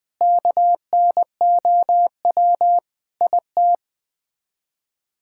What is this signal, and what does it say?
Morse code, heard as a keyed tone, about 21 wpm, 700 Hz. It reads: KNOW IT
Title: Sets of 8 Words - 20wpm